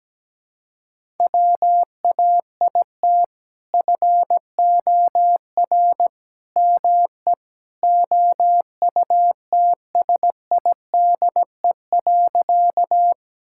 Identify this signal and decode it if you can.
Morse code, heard as a keyed tone, about 17 wpm, 700 Hz. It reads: WAIT FOR ME OUTSIDE.
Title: Sentences from Top 1000 Words - Review Part 6 - 17wpm